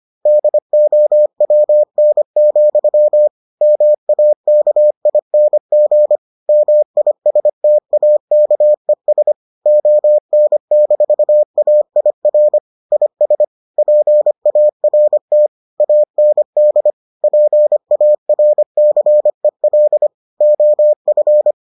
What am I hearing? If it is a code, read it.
Morse code, heard as a keyed tone, about 25 wpm, 600 Hz. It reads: DOWN, MAKING MISTAKES ON-AIR IS PART AND PARCEL OF